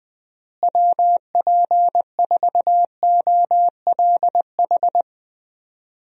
Morse code, heard as a keyed tone, about 20 wpm, 700 Hz. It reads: WP4OLH
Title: USA Call Signs - 6 Characters - Repeated 3x - 20wpm